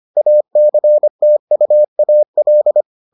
Morse code, heard as a keyed tone, about 25 wpm, 600 Hz. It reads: ACTUAL